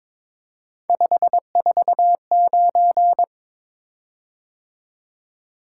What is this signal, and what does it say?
Morse code, heard as a keyed tone, about 22 wpm, 700 Hz. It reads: 549
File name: practice / 22wpm / Three Letter-Number Combinations - 22wpm